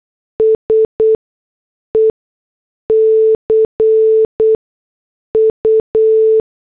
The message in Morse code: S E C U